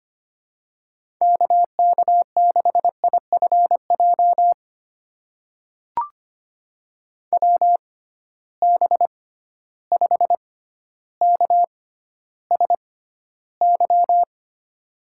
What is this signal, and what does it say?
Morse code, heard as a keyed tone, about 25 wpm, spaced as for 13 wpm, 700 Hz. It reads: KK6IFJ WB5KSY
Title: USA Call Signs - 6 Characters - 6x Character Spacing - 25wpm